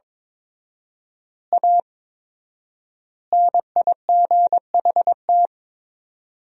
Morse code, heard as a keyed tone, about 22 wpm, 700 Hz. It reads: A NIGHT
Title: 4x Word-Spacing - Sentences from Top 300 Words - 22wpm